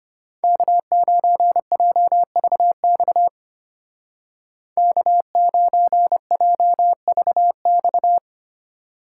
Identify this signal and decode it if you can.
Morse code, heard as a keyed tone, about 32 wpm, 700 Hz. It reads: K9JVX K9JVX